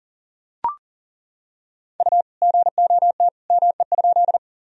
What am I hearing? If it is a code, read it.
Morse code, heard as a keyed tone, about 40 wpm, 700 Hz. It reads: U GOT ME?
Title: Sentences from Top 300 Words and Abbreviations 40wpm